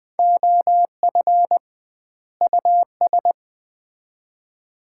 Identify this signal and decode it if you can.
Morse code, heard as a keyed tone, about 20 wpm, 700 Hz. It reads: OF US
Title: 2x Word-Spacing - Sets of 2 Words - 20wpm